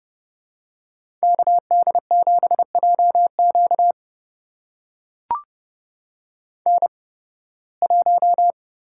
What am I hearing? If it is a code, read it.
Morse code, heard as a keyed tone, about 30 wpm, spaced as for 20 wpm, 700 Hz. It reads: KD7JQ N1
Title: USA Call Signs - 5 Characters - 8x Character Spacing - 30wpm